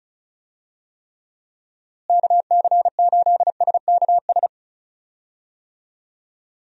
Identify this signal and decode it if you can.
Morse code, heard as a keyed tone, about 34 wpm, 700 Hz. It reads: KC8SKS
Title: USA Call Signs - 6 Characters - Repeated 3x - 35wpm